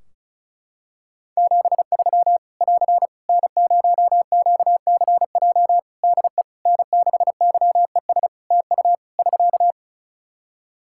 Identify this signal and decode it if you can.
Morse code, heard as a keyed tone, about 36 wpm, 700 Hz. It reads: 73 <AR> N0QCJ DE N6YES TU <SK>
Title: K1USN SST QSO Practice 35wpm